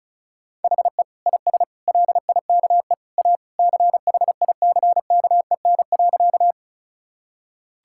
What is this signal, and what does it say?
Morse code, heard as a keyed tone, about 36 wpm, 690 Hz. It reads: HE IS LIKE A CHICKEN.
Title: Sentences from Top 1000 Words - Review Part 5 - 35wpm